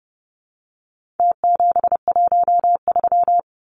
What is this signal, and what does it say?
Morse code, heard as a keyed tone, about 30 wpm, 700 Hz. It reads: T713
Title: Numbers - Contest Style - 4-Digit 30wpm